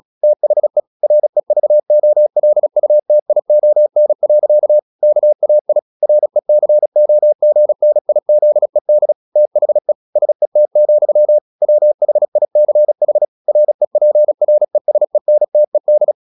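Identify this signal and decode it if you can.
Morse code, heard as a keyed tone, about 36 wpm, 600 Hz. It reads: THE REVOLUTION. KAI RECOGNIZED THE SET, WHICH REPRESENTED
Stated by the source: Morse practice at 36 wpm, text of Ken Follett's novel Never, from CW med Gnister.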